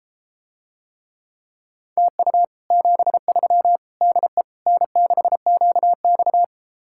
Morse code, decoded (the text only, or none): TU 73 DE N6QX